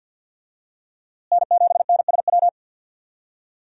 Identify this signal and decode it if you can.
Morse code, heard as a keyed tone, about 47 wpm, 700 Hz. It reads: N7NSW
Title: USA Call Signs - 5 Characters - Repeated 2x - 50wpm